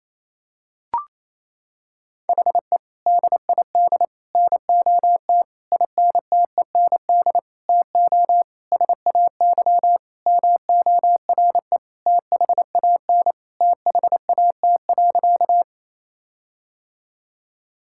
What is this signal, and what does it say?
Morse code, heard as a keyed tone, about 28 wpm, 700 Hz. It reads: HE DID NOT INTEND TO SAY MORE THAN THAT.
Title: Sentences from Top 1600 Words 28wpm